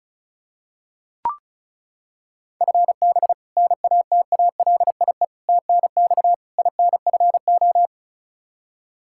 Morse code, heard as a keyed tone, about 36 wpm, 700 Hz. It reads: FB NATALIE TNX INFO
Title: Practice QSO - Exchange 3 - More Info - 35wpm